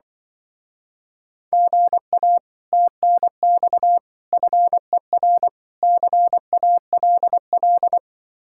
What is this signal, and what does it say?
Morse code, heard as a keyed tone, about 24 wpm, 700 Hz. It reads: GA TNX FER CALL
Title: Speed-Racing Common QSO Phrases - 20wpm